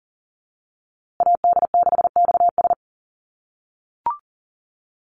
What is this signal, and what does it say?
Morse code, heard as a keyed tone, about 40 wpm, 700 Hz. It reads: AD6XS
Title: USA Call Signs Repeated 2x 40wpm